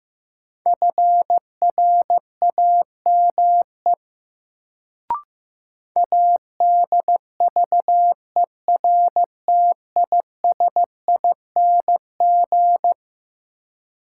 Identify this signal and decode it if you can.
Morse code, heard as a keyed tone, about 15 wpm, 700 Hz. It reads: FRAME ADVERTISING
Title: Sentences from Top 1000 Words - Review Part 6 - New Word Drill - 15wpm